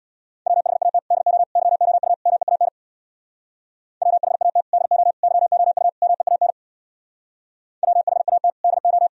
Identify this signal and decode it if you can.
Morse code, heard as a keyed tone, about 62 wpm, 700 Hz. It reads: WHAT DO YOU MEAN WHAT DO YOU MEAN WHAT DO
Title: Speed-Racing - Sets of 4 Words 50wpm